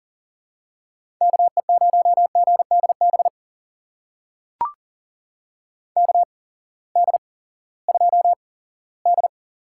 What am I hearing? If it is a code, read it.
Morse code, heard as a keyed tone, about 42 wpm, spaced as for 20 wpm, 700 Hz. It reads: KE0GDB KD2D
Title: USA Call Signs - 6 Characters - 8x Character Spacing - 40wpm